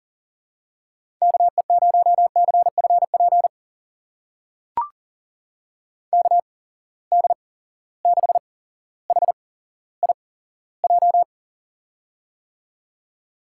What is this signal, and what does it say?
Morse code, heard as a keyed tone, about 40 wpm, spaced as for 17 wpm, 700 Hz. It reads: KE0CFP KD6HIJ
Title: USA Call Signs - 8x Character Spacing - 40wpm